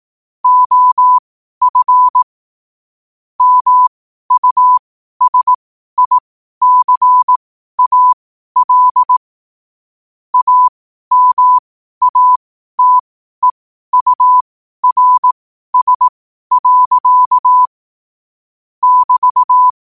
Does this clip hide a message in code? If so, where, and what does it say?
Yes: Morse code from 0.4 to 19.7 s, heard as a keyed tone, about 18 wpm, spaced as for 12 wpm, 1000 Hz. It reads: OF MUSICAL AMATEURS. <BT>